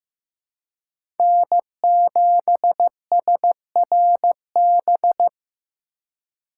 Morse code, heard as a keyed tone, about 18 wpm, 700 Hz. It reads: N7SRB
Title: Speed Racing - USA 1x3 Call Signs 15wpm